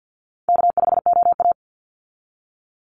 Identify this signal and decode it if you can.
Morse code, heard as a keyed tone, about 48 wpm, 700 Hz. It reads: K5OA
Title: Call Signs - 50wpm